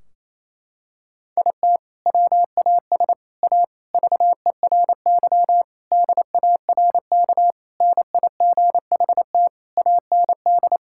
Morse code, heard as a keyed tone, about 28 wpm, 700 Hz. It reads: IT WAS A VERY DARK NIGHT AND
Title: Sentences from Top 1600 Words 28wpm